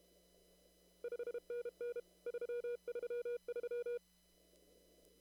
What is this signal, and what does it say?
Morse code, heard as a keyed tone, about 31 wpm, 480 Hz. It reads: 5NN 333
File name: CW / kt5j wpx cw-op-k5pi